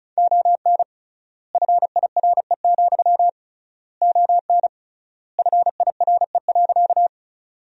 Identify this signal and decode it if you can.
Morse code, heard as a keyed tone, about 34 wpm, 700 Hz. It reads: ON FIRE, ON FIRE.
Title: 3x Word-Spacing - Sentences from Top 400 Words - 35wpm